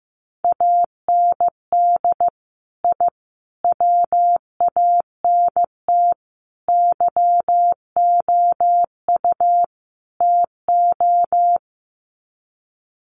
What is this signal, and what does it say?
Morse code, heard as a keyed tone, about 15 wpm, 700 Hz. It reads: AND I WANT YOU TO
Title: Sets of 5 Words Encore 15wpm